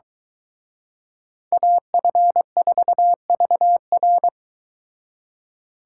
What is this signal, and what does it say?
Morse code, heard as a keyed tone, about 23 wpm, 700 Hz. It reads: AF4VR